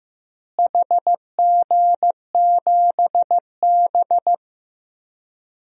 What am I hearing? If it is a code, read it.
Morse code, heard as a keyed tone, about 15 wpm, 700 Hz. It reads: HG7B
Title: DXCC-100 Call Signs - 4 Characters - Repeated 2x - 15wpm